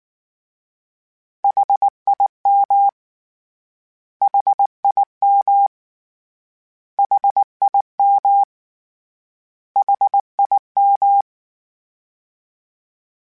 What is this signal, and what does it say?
Morse code, heard as a keyed tone, about 19 wpm, 800 Hz. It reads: HIM HIM HIM HIM
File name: Short Runs / Short Runs-mp3s / LongContrasts @18